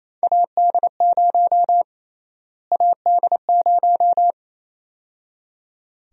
Morse code, heard as a keyed tone, about 28 wpm, 700 Hz. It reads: AD0 AD0
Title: USA Call Sign Prefixes - Repeated 2x - 28wpm